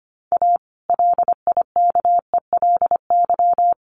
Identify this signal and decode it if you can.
Morse code, heard as a keyed tone, about 25 wpm, 700 Hz. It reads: A LIKELY
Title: Sentences from Top 600 Words 25wpm